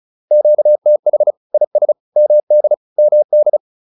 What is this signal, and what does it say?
Morse code, heard as a keyed tone, about 34 wpm, 600 Hz. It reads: QTH IS MD MD
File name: QSO 101 35